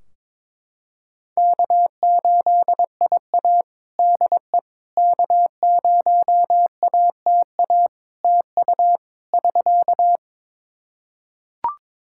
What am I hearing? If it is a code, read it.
Morse code, heard as a keyed tone, about 21 wpm, 700 Hz. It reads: K8IA DE K0ATA TU <SK>
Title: Practice QSO - Exchange 4 - Ending - 22wpm